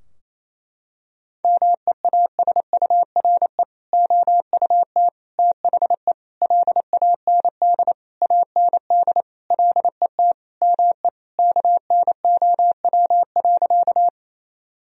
Morse code, heard as a keyed tone, about 28 wpm, 700 Hz. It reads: MEASURE OUT THE LAND AND LET ME KNOW.
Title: Sentences from Top 1000 Words - Review Part 2 - 28wpm